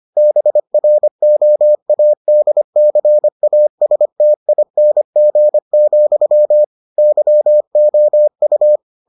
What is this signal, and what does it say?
Morse code, heard as a keyed tone, about 25 wpm, 600 Hz. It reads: BROADCASTING, YOU